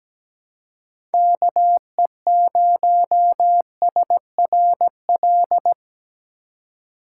Morse code, heard as a keyed tone, about 17 wpm, 700 Hz. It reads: KE0SRL
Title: K1USN SST QSO Practice - 17wpm